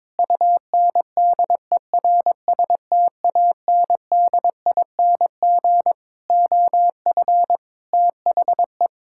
Morse code, heard as a keyed tone, about 22 wpm, 700 Hz. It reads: UNDERSTANDING OF THE
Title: Sentences from Top 1000 Words - Review Part 5 - 22wpm